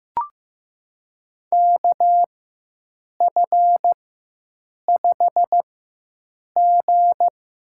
Morse code, heard as a keyed tone, about 15 wpm, spaced as for 7 wpm, 700 Hz. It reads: KF5G
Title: USA Call Signs - 4x Character Spacing - 15wpm